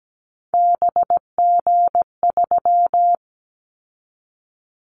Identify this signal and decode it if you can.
Morse code, heard as a keyed tone, about 17 wpm, 700 Hz. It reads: BG3
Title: Callsign Prefixes for Top 100 DXCC - 17wpm